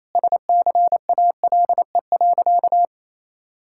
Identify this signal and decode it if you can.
Morse code, heard as a keyed tone, about 27 wpm, 700 Hz. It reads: SCALE.
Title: Sentences from Top 1000 Words - Review Part 5 - 28wpm